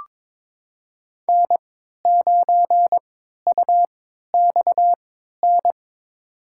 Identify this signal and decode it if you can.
Morse code, heard as a keyed tone, about 23 wpm, spaced as for 13 wpm, 700 Hz. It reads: N9UXN
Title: USA Call Signs - 5 Characters - 3x Character Spacing - 22wpm